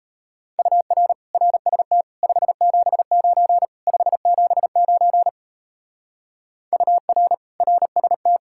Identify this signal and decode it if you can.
Morse code, heard as a keyed tone, about 34 wpm, 700 Hz. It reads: UR RST 579 579 UR RST
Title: Speed-Racing Common QSO Phrases Encore - 28wpm